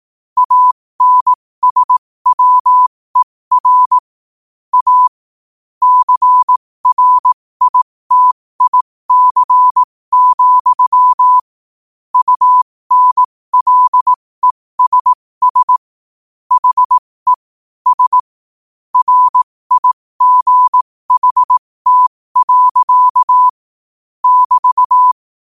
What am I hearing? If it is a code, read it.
Morse code, heard as a keyed tone, about 18 wpm, spaced as for 15 wpm, 1000 Hz. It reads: ANSWER A CRITIC, UNLESS HE S RIGHT. <BT>
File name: qotdfiles / 15wpm